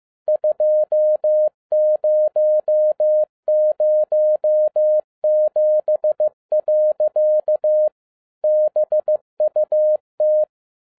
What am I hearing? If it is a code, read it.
Morse code, heard as a keyed tone, about 15 wpm, 610 Hz. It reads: 2007. BUT
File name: 2011-10-mp3 / n07-A DOT 15wpm.mp3